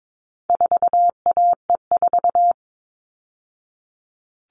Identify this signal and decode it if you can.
Morse code, heard as a keyed tone, about 22 wpm, 700 Hz. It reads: 4AE4
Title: All Cut Numbers - 4-Digit 22wpm